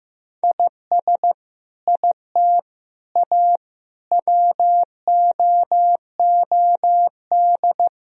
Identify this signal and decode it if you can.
Morse code, heard as a keyed tone, about 15 wpm, 700 Hz. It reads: IS IT A WOOD
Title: Sentences from Top 1000 Words - Review Part 2 - 15wpm